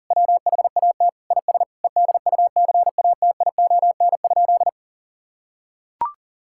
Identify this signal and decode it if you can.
Morse code, heard as a keyed tone, about 40 wpm, 700 Hz. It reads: WHAT IS EDUCATION?
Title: Sentences from Top 1600 Words 40wpm